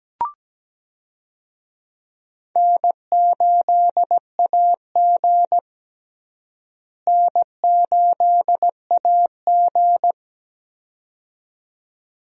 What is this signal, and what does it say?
Morse code, heard as a keyed tone, about 17 wpm, 700 Hz. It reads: N8AG N8AG
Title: USA Call Signs - 4 Characters - Repeated 2x - 17wpm